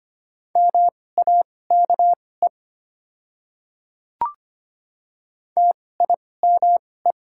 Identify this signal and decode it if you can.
Morse code, heard as a keyed tone, about 25 wpm, spaced as for 18 wpm, 690 Hz. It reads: MAKE TIME
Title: 2x Character Spacing - Top 200 Words - 25wpm